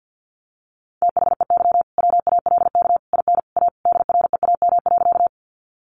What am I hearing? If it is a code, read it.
Morse code, heard as a keyed tone, about 50 wpm, 700 Hz. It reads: THEY WALK IN A DREAM.
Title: Sentences from Top 1500 Words 50wpm